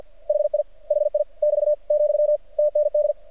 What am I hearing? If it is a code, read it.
Morse code, heard as a keyed tone, about 48 wpm, 600 Hz. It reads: HI HI <BT> , TND
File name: archived / CRSnet